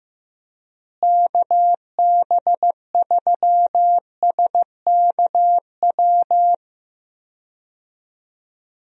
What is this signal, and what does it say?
Morse code, heard as a keyed tone, about 15 wpm, 700 Hz. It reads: KB3SKW
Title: USA Call Signs - 6 Characters - Repeated 3x - 15wpm